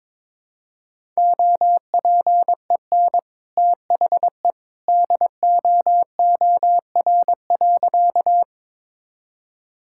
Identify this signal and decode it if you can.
Morse code, heard as a keyed tone, about 22 wpm, 700 Hz. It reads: OPEN THE DOOR.
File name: practice / 22wpm / Sentences with 2-Word Phrases - 22wpm